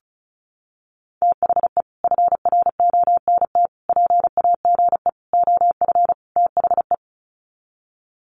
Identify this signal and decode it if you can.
Morse code, heard as a keyed tone, about 34 wpm, 700 Hz. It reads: THE FRONT PAGE OF THE
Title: Sets of 5 Words Encore 35wpm